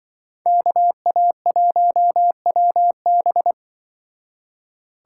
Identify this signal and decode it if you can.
Morse code, heard as a keyed tone, about 24 wpm, 700 Hz. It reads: KA1WB
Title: Speed Racing - USA 2x2 Call Signs 20wpm